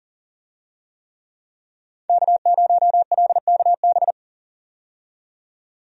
Morse code, heard as a keyed tone, about 40 wpm, 700 Hz. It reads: K0LKB
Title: USA Call Signs - 5 Characters - 40wpm